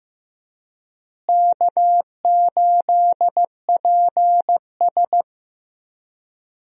Morse code, heard as a keyed tone, about 16 wpm, 700 Hz. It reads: K8PS
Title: USA Call Signs - 4 Characters - Repeated 3x - 15wpm